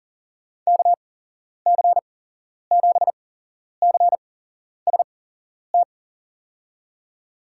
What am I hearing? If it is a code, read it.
Morse code, heard as a keyed tone, about 37 wpm, spaced as for 11 wpm, 700 Hz. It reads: KC7CST
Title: USA Call Signs - 6 Characters - 8x Character Spacing - 40wpm